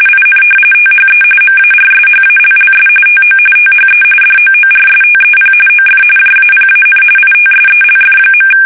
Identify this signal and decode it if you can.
RTTY, about 45.45 baud. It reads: IN THE MARSHALLS TO THE DOORSTEP OF JAPAN ITSELF.
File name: fsk850 / battleshipnj